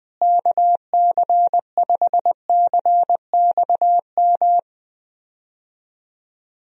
Morse code, heard as a keyed tone, about 20 wpm, 700 Hz. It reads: KC5CXM
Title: USA Call Signs - 6 Characters - Repeated 3x - 20wpm